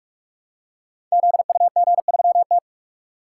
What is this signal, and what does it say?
Morse code, heard as a keyed tone, about 44 wpm, 690 Hz. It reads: ZUG3T
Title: Code Groups - 5 Letter:Numbers - 45wpm